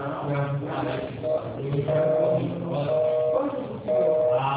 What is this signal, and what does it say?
Busy tone (North America), 480 + 620 Hz, with speech.